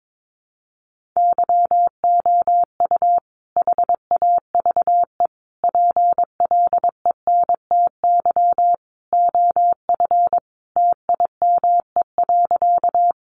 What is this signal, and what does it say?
Morse code, heard as a keyed tone, about 22 wpm, 700 Hz. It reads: YOU HAVE PLENTY OF TIME.
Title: Sentences from Top 1300 Words 22wpm